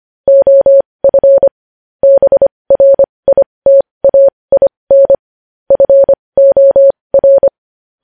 Morse code, heard as a keyed tone, about 25 wpm, spaced as for 20 wpm, 550 Hz. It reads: OF BRITAIN FOR
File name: podcast / News headlines-1x-25x20